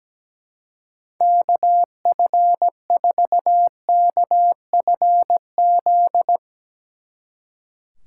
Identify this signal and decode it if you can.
Morse code, heard as a keyed tone, about 17 wpm, 700 Hz. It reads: KF4KFZ
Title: Speed Racing - USA 2x3 Call Signs 17wpm